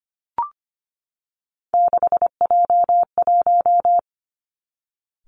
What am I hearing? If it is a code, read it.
Morse code, heard as a keyed tone, about 25 wpm, 700 Hz. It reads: 6J1
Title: Three Challenging Letter-Number Combinations 25wpm